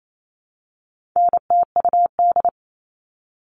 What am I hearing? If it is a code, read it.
Morse code, heard as a keyed tone, about 28 wpm, 700 Hz. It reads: NTUD
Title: All Cut Numbers - 4-Digit 28wpm